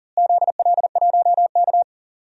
Morse code, heard as a keyed tone, about 40 wpm, 700 Hz. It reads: ZL1K
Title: DXCC-100 Call Signs - 4 Characters - Repeated 2x - 40wpm